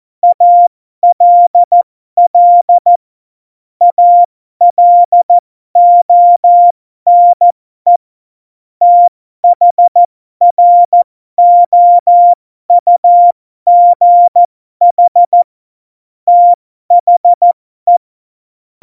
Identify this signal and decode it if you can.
Morse code, heard as a keyed tone, about 14 wpm, spaced as for 12 wpm, 700 Hz. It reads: ALL ALONE THROUGH THE
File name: Morsekurs / Witze